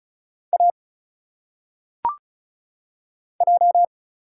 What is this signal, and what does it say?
Morse code, heard as a keyed tone, about 36 wpm, 700 Hz. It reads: A J
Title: Single Letters - Phonetic Pronunciation 35wpm